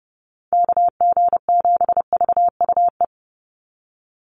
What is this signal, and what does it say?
Morse code, heard as a keyed tone, about 30 wpm, 700 Hz. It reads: KG7VUE
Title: USA Call Signs 30wpm